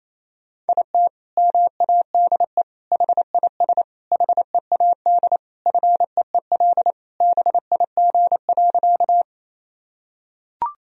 Morse code, heard as a keyed tone, about 28 wpm, 700 Hz. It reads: IT MADE HIS HEAD FEEL BIG.
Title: Sentences from Top 1000 Words - Review Part 1 - 28wpm